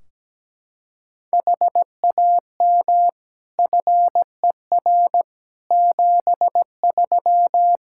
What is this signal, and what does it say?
Morse code, heard as a keyed tone, about 17 wpm, 700 Hz. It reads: HAM FER 73
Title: Practice QSO - Exchange 3 - More Info - 17wpm